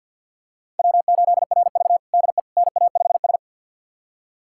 Morse code, heard as a keyed tone, about 49 wpm, 700 Hz. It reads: W8RV DE NA5S
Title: USA Callsign DE Callsign - 4 Characters - 50wpm